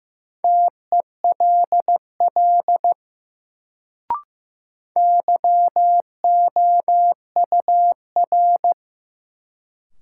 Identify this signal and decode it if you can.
Morse code, heard as a keyed tone, about 15 wpm, 700 Hz. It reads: TELL YOUR
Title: Sets of 2 Words - New Word Drill - 15wpm